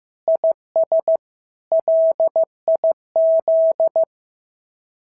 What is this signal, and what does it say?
Morse code, heard as a keyed tone, about 15 wpm, 690 Hz. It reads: IS LIZ
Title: SOTA QSO Practice 15wpm